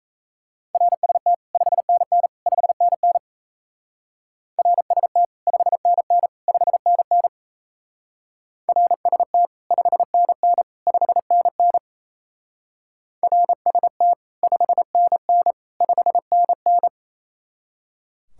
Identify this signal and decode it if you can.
Morse code, heard as a keyed tone, about 36 wpm, 700 Hz. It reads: RST 5NN 5NN RST 5NN 5NN RST 5NN 5NN RST 5NN 5NN